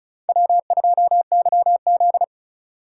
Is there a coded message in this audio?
Yes: Morse code from 0.3 to 2.2 s, heard as a keyed tone, about 45 wpm, 700 Hz. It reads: W2YZ